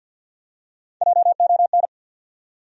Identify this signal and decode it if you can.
Morse code, heard as a keyed tone, about 49 wpm, 700 Hz. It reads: JON
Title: USA Common Names - Max 3 Letters - 50wpm